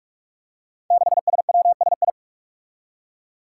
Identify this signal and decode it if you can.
Morse code, heard as a keyed tone, about 45 wpm, 700 Hz. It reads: 6SWSI
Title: Five Challenging Letter-Number Combinations 45wpm